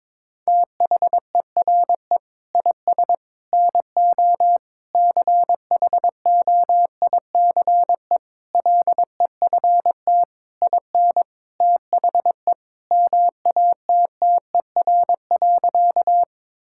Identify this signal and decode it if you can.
Morse code, heard as a keyed tone, about 22 wpm, 700 Hz. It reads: THERE IS NO CHOICE LEFT IN THE MATTER.
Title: Sentences from Top 1000 Words - Review Part 6 - 22wpm